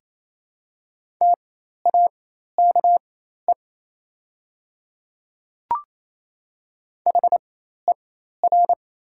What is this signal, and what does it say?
Morse code, heard as a keyed tone, about 28 wpm, spaced as for 13 wpm, 700 Hz. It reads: TAKE HER
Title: 4x Character Spacing - Top 200 Words - 28wpm